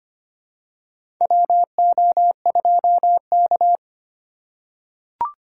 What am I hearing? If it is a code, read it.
Morse code, heard as a keyed tone, about 25 wpm, 700 Hz. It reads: WO2K
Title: USA Call Signs - 4 Characters - 6x Character Spacing - 25wpm